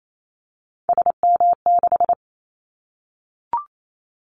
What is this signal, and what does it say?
Morse code, heard as a keyed tone, about 28 wpm, 700 Hz. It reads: SM6
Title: Callsign Prefixes for Top 25 DXCC - Repeated 3x - 28wpm